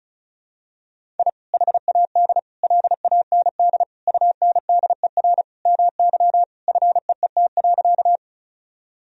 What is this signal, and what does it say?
Morse code, heard as a keyed tone, about 36 wpm, 700 Hz. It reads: I HAD LAND UNDER MY FEET.
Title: Sentences from Top 1000 Words - Review Part 1 - 35wpm